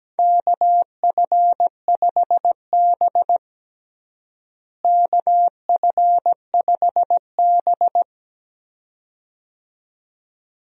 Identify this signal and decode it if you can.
Morse code, heard as a keyed tone, about 17 wpm, 700 Hz. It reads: KF5B KF5B
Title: USA Call Signs - 4 Characters - Repeated 2x - 17wpm